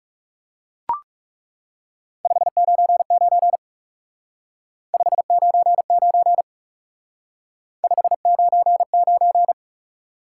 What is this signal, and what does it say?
Morse code, heard as a keyed tone, about 38 wpm, 700 Hz. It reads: 599 599 599